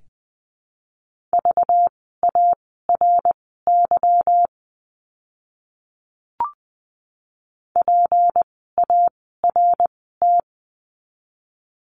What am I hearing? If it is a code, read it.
Morse code, heard as a keyed tone, about 20 wpm, spaced as for 14 wpm, 700 Hz. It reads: VARY PART
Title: 192 Lesson 31f - Letter v - Words 20wpm